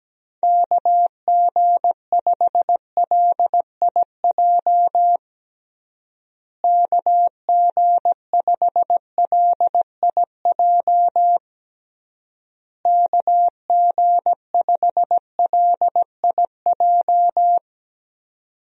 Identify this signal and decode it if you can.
Morse code, heard as a keyed tone, about 17 wpm, 700 Hz. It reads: KG5LIJ KG5LIJ KG5LIJ
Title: USA Call Signs - 6 Characters - Repeated 3x - 17wpm